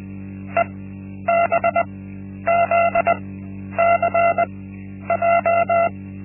Morse code, heard as a keyed tone, about 20 wpm, spaced as for 10 wpm, 680 Hz. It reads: EBZCJ
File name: Linksmakalnis-1